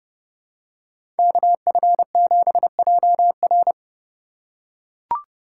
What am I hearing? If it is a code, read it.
Morse code, heard as a keyed tone, about 30 wpm, 700 Hz. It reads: KF7JR
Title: USA Call Signs - 5 Characters - 2x Character Spacing - 30wpm